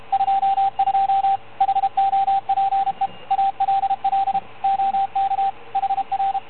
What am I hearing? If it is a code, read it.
Morse code, heard as a keyed tone, about 33 wpm, 780 Hz, with radio noise. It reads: 22 HOPE ALL OK HR